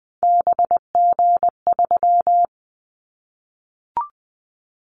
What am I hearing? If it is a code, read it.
Morse code, heard as a keyed tone, about 20 wpm, 700 Hz. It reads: BG3
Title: Callsign Prefixes for Top 100 DXCC - 20wpm